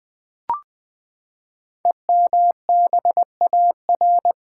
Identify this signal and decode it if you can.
Morse code, heard as a keyed tone, about 20 wpm, 700 Hz. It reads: EMBAR
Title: Build A Word - 5 Letters At A Time - 12-Letter Words - 20wpm